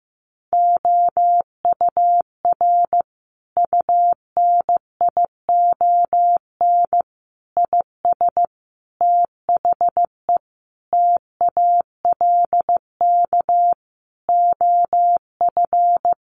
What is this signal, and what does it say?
Morse code, heard as a keyed tone, about 15 wpm, 700 Hz. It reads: OUR UNION IS THE TALK OF
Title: Sentences from Top 1400 Words 15wpm